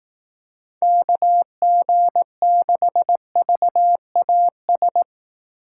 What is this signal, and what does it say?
Morse code, heard as a keyed tone, about 18 wpm, 700 Hz. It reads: KG6VAS
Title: Speed Racing - USA Call Signs 15wpm